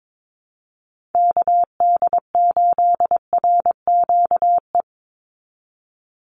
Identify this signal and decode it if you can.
Morse code, heard as a keyed tone, about 22 wpm, 700 Hz. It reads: KD8RQE
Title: Call Signs Repeated 2x - 22wpm